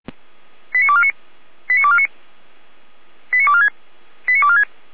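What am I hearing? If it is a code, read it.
Five-tone selective call (ZVEI), heard as quick runs of five tones, tones of about 70 ms. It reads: 89249 89249 89246 89246
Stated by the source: ZVEI five-tone selective call received on SDR, feuerwehr-orlamuende.org fire brigade audio server.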